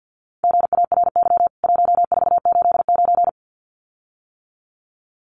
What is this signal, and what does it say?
Morse code, heard as a keyed tone, about 48 wpm, 700 Hz. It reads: GARY 1489
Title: CWT Exchange 50wpm